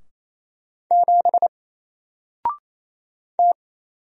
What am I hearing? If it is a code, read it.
Morse code, heard as a keyed tone, about 27 wpm, 700 Hz. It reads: 7 T
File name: Single-Letter-Number-28wpm